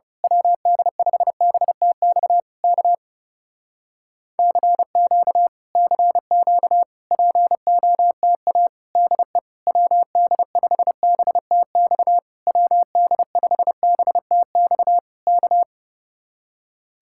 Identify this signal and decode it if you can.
Morse code, heard as a keyed tone, about 28 wpm, 700 Hz. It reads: WD5BTX K CQ CQ POTA DE WD5BTX WD5BTX K
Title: Speed-Racing Common QSO Phrases - 30wpm